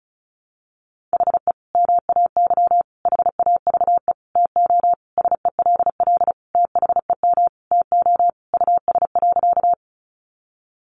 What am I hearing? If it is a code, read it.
Morse code, heard as a keyed tone, about 34 wpm, 700 Hz. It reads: HE MAY HAVE TO SELL THEM TO US.
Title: Sentences from Top 1100 Words 35wpm